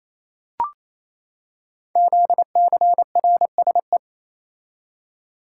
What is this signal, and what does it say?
Morse code, heard as a keyed tone, about 27 wpm, 700 Hz. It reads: ZCRSE
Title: Code Groups - 5 Letters - 28wpm